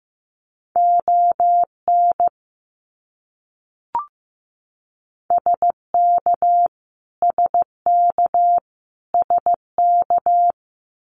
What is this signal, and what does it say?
Morse code, heard as a keyed tone, about 15 wpm, 700 Hz. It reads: ON SK SK SK
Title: Canadian Provinces and Territories - Repeated 3x - 15wpm